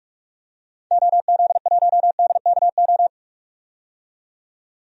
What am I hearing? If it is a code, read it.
Morse code, heard as a keyed tone, about 42 wpm, 700 Hz. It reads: OZ1DKO
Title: DXCC-100 Call Signs - 6 Characters - 45wpm